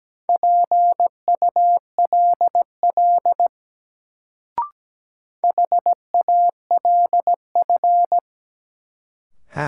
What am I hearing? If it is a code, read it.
Morse code, heard as a keyed tone, about 17 wpm, 700 Hz. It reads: PULL HALF
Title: Sentences from Top 400 Words - New Word Drill - 17wpm